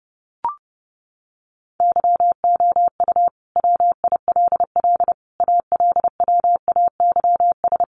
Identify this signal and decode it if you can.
Morse code, heard as a keyed tone, about 28 wpm, 700 Hz. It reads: YOU WILL ALWAYS
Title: Sentences from Top 400 Words 30wpm